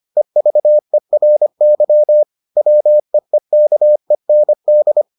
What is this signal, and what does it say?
Morse code, heard as a keyed tone, about 25 wpm, 600 Hz. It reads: EVERY WEEKEND